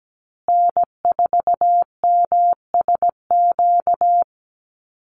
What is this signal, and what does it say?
Morse code, heard as a keyed tone, about 17 wpm, 700 Hz. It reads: N4MSQ
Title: USA Call Signs Repeated 2x 17wpm